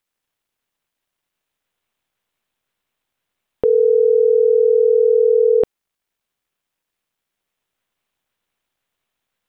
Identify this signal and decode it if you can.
Ringback tone (North America), 440 + 480 Hz.